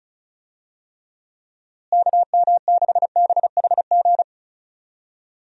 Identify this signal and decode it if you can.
Morse code, heard as a keyed tone, about 34 wpm, 700 Hz. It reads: KM6BHG
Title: USA Call Signs - 6 Characters - Repeated 3x - 35wpm